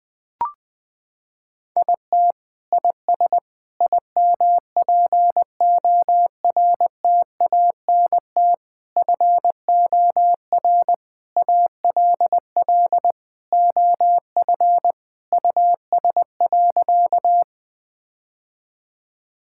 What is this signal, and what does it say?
Morse code, heard as a keyed tone, about 21 wpm, 700 Hz. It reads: IT IS IMPORTANT FOR ALL OF US.
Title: Sentences with 4-Word Phrases - 20wpm